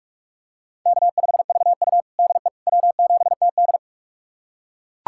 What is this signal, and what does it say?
Morse code, heard as a keyed tone, about 44 wpm, 690 Hz. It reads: K5VU DE W7TD
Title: USA Callsign DE Callsign - 4 Characters - 45wpm